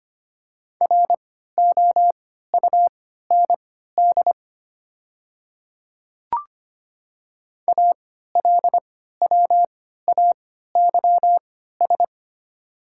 Morse code, heard as a keyed tone, about 25 wpm, spaced as for 14 wpm, 700 Hz. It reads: ROUND ALWAYS
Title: 3x Character Spacing - Top 300 Words - 25wpm